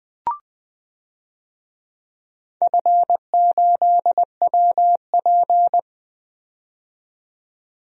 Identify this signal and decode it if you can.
Morse code, heard as a keyed tone, about 20 wpm, 700 Hz. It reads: F8WP
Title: DXCC-100 Call Signs - 4 Characters - 20wpm